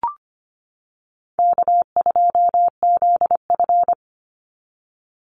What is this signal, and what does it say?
Morse code, heard as a keyed tone, about 25 wpm, 700 Hz. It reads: K2ZF
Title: Easy Call Signs - 25wpm